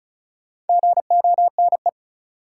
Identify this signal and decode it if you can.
Morse code, heard as a keyed tone, about 36 wpm, 700 Hz. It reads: GONE
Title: Sentences from Top 1200 Words - New Word Drill Alt - 35wpm